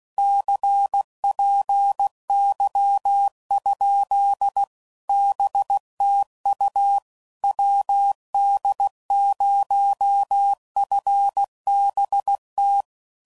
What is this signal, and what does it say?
Morse code, heard as a keyed tone, about 16 wpm, 800 Hz. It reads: CPY? BTU WD0FBT
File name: CW Code Groups @15WPM / QSOs / WD0FBT